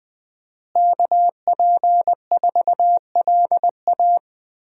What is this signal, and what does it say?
Morse code, heard as a keyed tone, about 20 wpm, 700 Hz. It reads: KP4LA
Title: USA Call Signs - 5 Characters - Repeated 2x - 20wpm